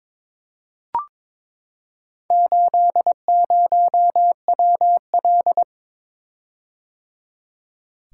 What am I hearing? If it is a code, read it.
Morse code, heard as a keyed tone, about 22 wpm, 700 Hz. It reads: 80WL
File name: practice / 22wpm / Four Letter-Number Combinations - 22wpm